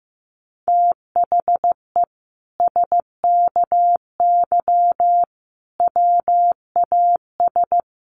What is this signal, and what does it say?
Morse code, heard as a keyed tone, about 15 wpm, 700 Hz. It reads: THE SKY WAS
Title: Sentences from Top 1200 Words 15wpm